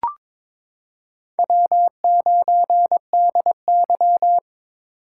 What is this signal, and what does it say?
Morse code, heard as a keyed tone, about 22 wpm, 700 Hz. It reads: W9DY
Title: K1USN SST QSO Practice - 22wpm